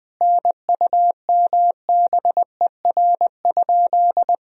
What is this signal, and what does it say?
Morse code, heard as a keyed tone, about 20 wpm, 700 Hz. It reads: NUMBER?
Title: Sentences from Top 1000 Words - Review Part 1 - 20wpm